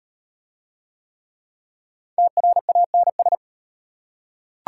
Morse code, heard as a keyed tone, about 35 wpm, 690 Hz. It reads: TRANS